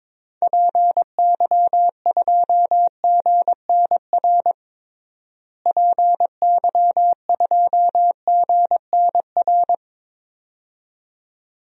Morse code, heard as a keyed tone, about 22 wpm, 700 Hz. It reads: PY2GNR PY2GNR
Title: DXCC-100 Call Signs - 6 Characters - Repeated 2x - 22wpm